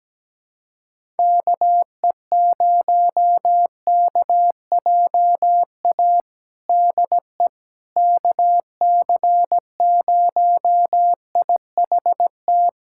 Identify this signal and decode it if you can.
Morse code, heard as a keyed tone, about 17 wpm, 700 Hz. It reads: KE0KJA DE KC0IHT
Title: USA Callsign DE Callsign - 6 Characters - 17wpm